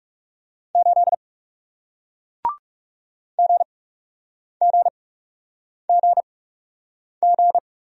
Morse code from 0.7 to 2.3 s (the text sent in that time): 8